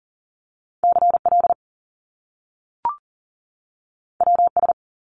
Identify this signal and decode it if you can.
Morse code, heard as a keyed tone, about 39 wpm, 700 Hz. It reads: CL WS
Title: Top Two Letter Combinations in Words 40wpm